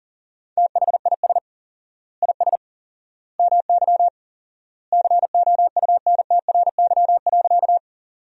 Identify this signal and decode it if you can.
Morse code, heard as a keyed tone, about 40 wpm, 700 Hz. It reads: THIS IS MY COUNTRY.
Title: 4x Word-Spacing - Sentences from Top 300 Words - 40wpm